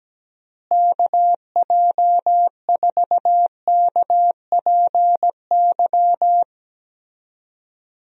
Morse code, heard as a keyed tone, about 17 wpm, 700 Hz. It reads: KJ4KPY
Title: USA Call Signs - 6 Characters - Repeated 2x - 17wpm